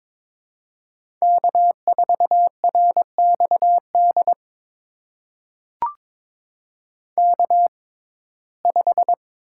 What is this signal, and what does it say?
Morse code, heard as a keyed tone, about 22 wpm, spaced as for 16 wpm, 700 Hz. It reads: K4RXD K5
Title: USA Call Signs - 5 Characters - 6x Character Spacing - 22wpm